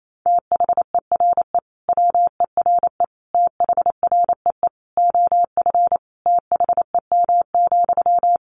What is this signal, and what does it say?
Morse code, heard as a keyed tone, about 28 wpm, 700 Hz. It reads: THERE WERE THREE OF THEM,
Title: Sentences from Top 800 Words 28wpm